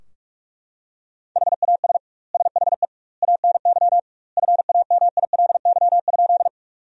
Morse code, heard as a keyed tone, about 41 wpm, 700 Hz. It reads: HAS SHE ANY FAMILY?
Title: 2x Word-Spacing - Sentences from Top 400 Words - 45wpm